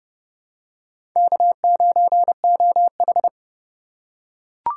Morse code, heard as a keyed tone, about 31 wpm, 700 Hz. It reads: K9OH
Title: USA Call Signs - 2x Character Spacing - 30wpm